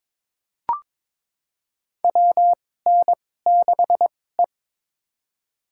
Morse code, heard as a keyed tone, about 23 wpm, spaced as for 16 wpm, 700 Hz. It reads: WN6E
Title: USA Call Signs - 4 Characters - 2x Character Spacing - 22wpm